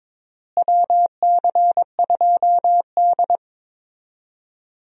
Morse code, heard as a keyed tone, about 22 wpm, 700 Hz. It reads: WC2D
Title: USA Call Signs - 4 Characters - Repeated 3x - 22wpm